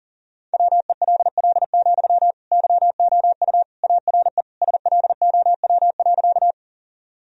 Morse code, heard as a keyed tone, about 37 wpm, 700 Hz. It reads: WELL, YOU ARE SLOW.
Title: Sentences from Top 1000 Words - Review Part 2 - 40wpm